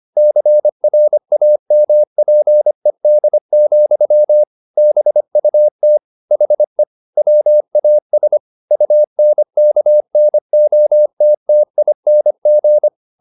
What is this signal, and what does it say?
Morse code, heard as a keyed tone, about 25 wpm, 600 Hz. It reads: CRAMPED, BUT HE WAS UNKNOTTING
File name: the old man and the sea part 1 25wpm